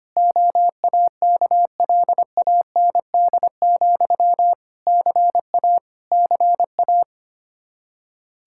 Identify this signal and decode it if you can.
Morse code, heard as a keyed tone, about 25 wpm, 700 Hz. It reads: OAKLAND, CA CA
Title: K1USN SST QSO Practice 25wpm